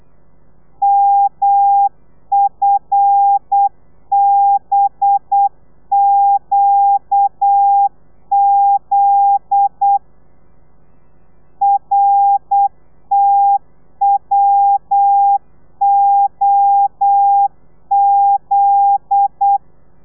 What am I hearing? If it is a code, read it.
Morse code, heard as a keyed tone, about 8 wpm, 800 Hz. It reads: MFBQZ RTWOZ
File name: telegrafia / i2tfj / Lez9 35